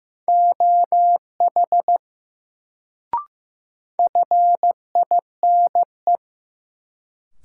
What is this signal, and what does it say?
Morse code, heard as a keyed tone, about 15 wpm, 700 Hz. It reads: OH FINE
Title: Sentences from Top 500 Words - New Word Drill - 15wpm